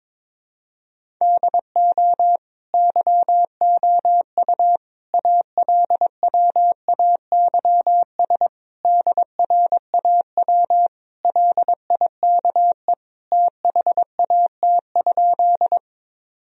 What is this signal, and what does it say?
Morse code, heard as a keyed tone, about 22 wpm, 700 Hz. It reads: DO YOU ALWAYS DRAW LIKE THAT?
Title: Sentences from Top 1000 Words - Review Part 1 - 22wpm